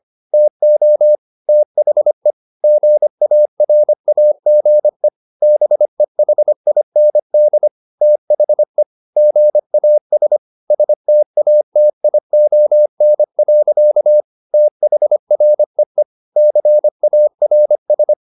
Morse code, heard as a keyed tone, about 25 wpm, 600 Hz. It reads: TO THE GARAGE BEHIND THE GAS STATION. THREE CARS